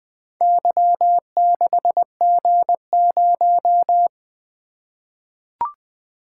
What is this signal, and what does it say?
Morse code, heard as a keyed tone, about 20 wpm, 700 Hz. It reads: Y6G0
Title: Four Letter-Number Combinations - 20wpm